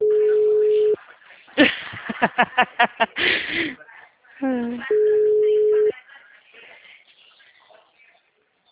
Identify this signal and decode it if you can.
Ringback tone (Europe, 425 Hz), with speech.
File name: out- -2011- - - .514